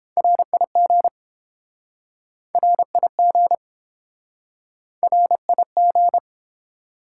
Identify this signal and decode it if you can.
Morse code, heard as a keyed tone, about 30 wpm, 700 Hz. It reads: RIG RIG RIG